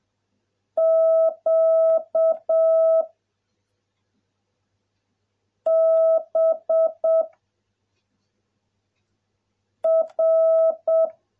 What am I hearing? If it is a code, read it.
Morse code, heard as a keyed tone, about 7 wpm, 660 Hz. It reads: Q B R